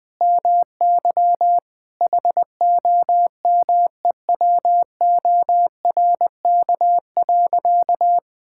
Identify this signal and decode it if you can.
Morse code, heard as a keyed tone, about 20 wpm, 700 Hz. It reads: MY HOMEWORK.
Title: Sentences with 3-Word Phrases - 20wpm